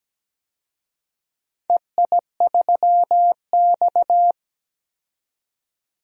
Morse code, heard as a keyed tone, about 17 wpm, 700 Hz. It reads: EI3X
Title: DXCC-100 Call Signs - 4 Characters - 17wpm